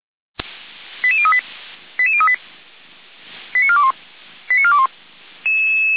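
Five-tone selective call (ZVEI), heard as quick runs of five tones, tones of about 70 ms. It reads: 88038 88038 89421 89421
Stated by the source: ZVEI five-tone call 88367 received on an SDR.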